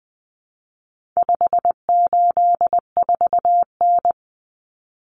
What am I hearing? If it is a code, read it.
Morse code, heard as a keyed tone, about 20 wpm, 700 Hz. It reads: 584N
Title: Common Cut Numbers - 4-Digit 20wpm